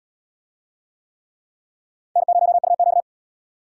Morse code, heard as a keyed tone, about 63 wpm, 700 Hz. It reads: N0UQ